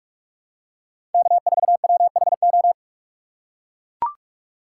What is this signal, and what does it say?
Morse code, heard as a keyed tone, about 43 wpm, 690 Hz. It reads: K4WHO
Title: USA Call Signs - 4x Character Spacing - 45wpm